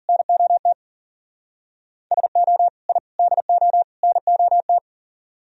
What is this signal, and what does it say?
Morse code, heard as a keyed tone, about 49 wpm, 700 Hz. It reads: NOT SO I DO NOT
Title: Speed-Racing - Sets of 4 Words 40wpm